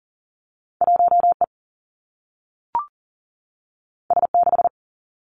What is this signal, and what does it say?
Morse code, heard as a keyed tone, about 40 wpm, 700 Hz. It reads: 1 E S 6
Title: Two Challenging Letter-Number Combinations 40wpm